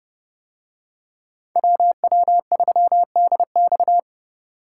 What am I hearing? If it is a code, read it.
Morse code, heard as a keyed tone, about 29 wpm, 700 Hz. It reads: WW3DX